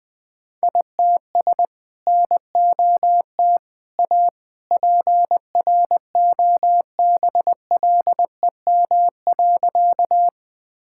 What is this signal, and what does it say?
Morse code, heard as a keyed tone, about 20 wpm, 700 Hz. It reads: ITS NOT A PROBLEM.
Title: Sentences with 2-Word Phrases - 20wpm